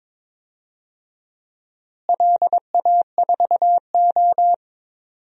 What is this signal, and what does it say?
Morse code, heard as a keyed tone, about 22 wpm, 700 Hz. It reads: LA4O
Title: DXCC-100 Call Signs - 4 Characters - Repeated 3x - 22wpm